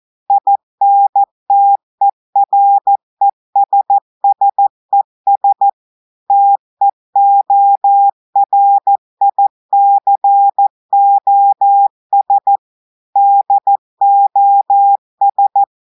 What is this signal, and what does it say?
Morse code, heard as a keyed tone, about 14 wpm, 800 Hz. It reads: INTERESSES TEORICOS DOS
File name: audios / portugais / portugais texte en clair 1 vitesse 14 mots minutes